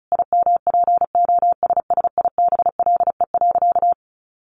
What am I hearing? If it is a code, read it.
Morse code, heard as a keyed tone, about 34 wpm, 700 Hz. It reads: IMPOSSIBLE.
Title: Sentences from Top 1100 Words 35wpm